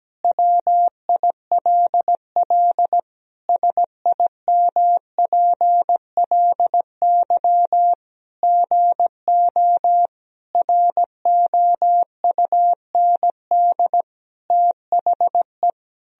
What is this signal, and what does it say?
Morse code, heard as a keyed tone, about 17 wpm, 700 Hz. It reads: WILL SIMPLY GO ROUND THE
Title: Sentences from Top 1000 Words - Review Part 4 - 17wpm